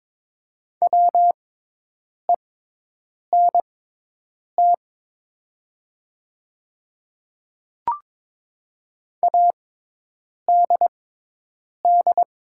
Morse code, heard as a keyed tone, about 22 wpm, spaced as for 8 wpm, 700 Hz. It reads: WENT ADD
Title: 6x Character Spacing - Top 200 Words - 22wpm